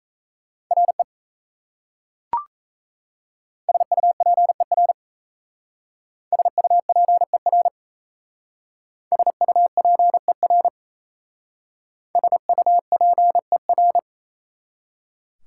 Morse code, heard as a keyed tone, about 35 wpm, 700 Hz. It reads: RE SUPER SUPER SUPER SUPER